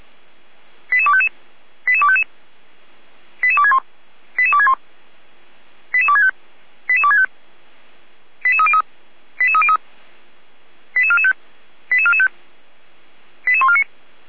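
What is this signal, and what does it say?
Five-tone selective call (ZVEI), heard as quick runs of five tones, tones of about 70 ms. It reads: 88255 88255 88261 88261 88265 88265 88303 88303 88445 88445 88149
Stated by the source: ZVEI five-tone selective call received by SDR.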